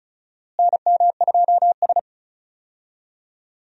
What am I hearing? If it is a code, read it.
Morse code, heard as a keyed tone, about 34 wpm, 700 Hz. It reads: NM2S
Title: USA Call Signs - 4 Characters - Repeated 3x - 35wpm